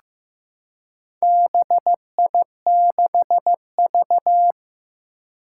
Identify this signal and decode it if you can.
Morse code, heard as a keyed tone, about 15 wpm, 700 Hz. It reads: BI6V
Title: Four Challenging Letter-Number Combinations 15wpm